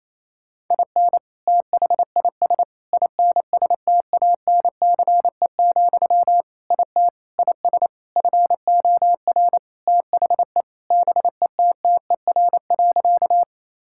Morse code, heard as a keyed tone, about 28 wpm, 700 Hz. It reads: IN THIS INSTANCE, IT IS FOR THE BETTER.
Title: Sentences from Top 1000 Words - Review Part 5 - 28wpm